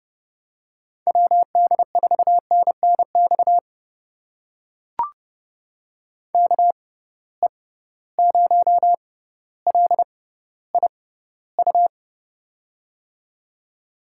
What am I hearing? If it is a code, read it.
Morse code, heard as a keyed tone, about 29 wpm, spaced as for 15 wpm, 700 Hz. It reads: WD4NNX KE0LIU
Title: USA Call Signs - 6 Characters - 6x Character Spacing - 30wpm